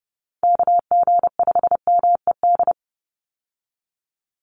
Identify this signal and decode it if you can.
Morse code, heard as a keyed tone, about 30 wpm, 700 Hz. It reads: KG5MED
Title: USA Call Signs 30wpm